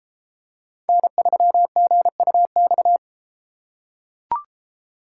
Morse code, heard as a keyed tone, about 32 wpm, 700 Hz. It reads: N3GUX